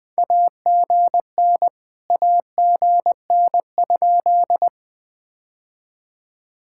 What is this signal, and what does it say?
Morse code, heard as a keyed tone, about 20 wpm, 700 Hz. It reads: AGN AGN?
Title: POTA QSO Practice - 20wpm